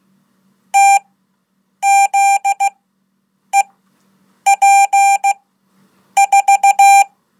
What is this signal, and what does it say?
Morse code, heard as a keyed tone, about 16 wpm, spaced as for 8 wpm, 790 Hz. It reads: TZEP4
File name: Code / 5min-1